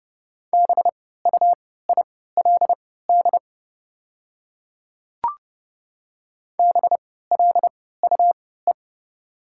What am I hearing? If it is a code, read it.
Morse code, heard as a keyed tone, about 30 wpm, spaced as for 17 wpm, 700 Hz. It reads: BUILD BLUE
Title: 3x Character Spacing - Top 500 Words - 30wpm